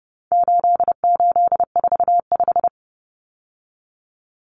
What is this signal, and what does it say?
Morse code, heard as a keyed tone, about 30 wpm, 700 Hz. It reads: 8845
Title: Numbers - 4-Digit 30wpm